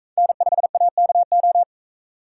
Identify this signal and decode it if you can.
Morse code, heard as a keyed tone, about 40 wpm, 700 Hz. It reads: N5AKO